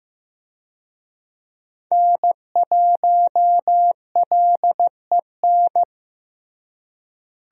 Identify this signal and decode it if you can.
Morse code, heard as a keyed tone, about 15 wpm, 700 Hz. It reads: N1LEN
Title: USA Call Signs - 5 Characters - Repeated 3x - 15wpm